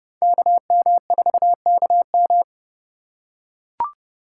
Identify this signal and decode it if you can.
Morse code, heard as a keyed tone, about 29 wpm, 700 Hz. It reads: KM4KM